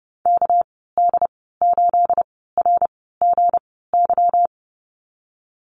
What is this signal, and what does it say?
Morse code, heard as a keyed tone, about 30 wpm, spaced as for 17 wpm, 700 Hz. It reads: KD8RGY
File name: Lesson 37g - Letter x - Call Signs 30wpm